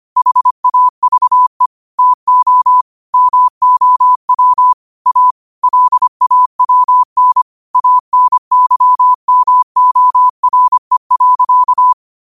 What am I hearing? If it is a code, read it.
Morse code, heard as a keyed tone, about 25 wpm, 1000 Hz. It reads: HAVE TO MOW A LAWN ANYMORE.
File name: qotdfiles / 25wpm